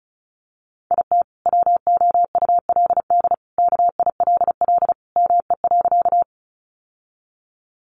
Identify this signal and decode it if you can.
Morse code, heard as a keyed tone, about 34 wpm, 700 Hz. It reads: IT WOULD KILL ME.
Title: Sentences from Top 1400 Words 35wpm